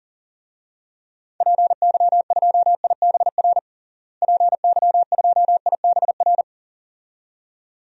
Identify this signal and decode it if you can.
Morse code, heard as a keyed tone, about 40 wpm, 700 Hz. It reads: PY2IBR PY2IBR
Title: DXCC-100 Call Signs - 6 Characters - Repeated 2x - 40wpm